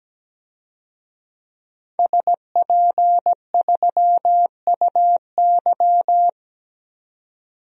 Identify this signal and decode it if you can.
Morse code, heard as a keyed tone, about 17 wpm, 700 Hz. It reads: SP3UY
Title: DXCC-100 Call Signs - 5 Characters - 17wpm